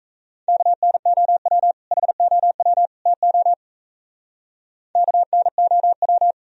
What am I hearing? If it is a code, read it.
Morse code, heard as a keyed tone, about 35 wpm, 700 Hz. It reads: KNOW HOW TO KNOW